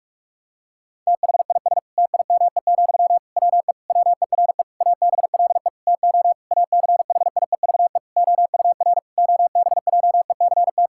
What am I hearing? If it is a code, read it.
Morse code, heard as a keyed tone, about 46 wpm, 700 Hz. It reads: THIS TIME, WE WERE ABLE TO ACHIEVE OUR OBJECT
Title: Sentences from Top 1000 Words - Review Part 6 - 45wpm